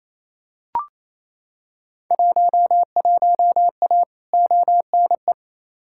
Morse code, heard as a keyed tone, about 28 wpm, 700 Hz. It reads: 11A ONE
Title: Field Day Exchanges Encore - 28wpm